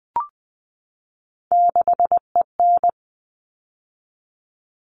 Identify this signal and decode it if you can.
Morse code, heard as a keyed tone, about 20 wpm, 700 Hz. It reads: 6EN
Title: All Cut Numbers - 3-Digit 20wpm